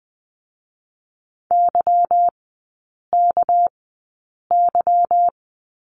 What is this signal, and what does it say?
Morse code, heard as a keyed tone, about 20 wpm, spaced as for 8 wpm, 700 Hz. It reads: YKY
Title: 218 Lesson 35e - Number 0 - Three Characters 20wpm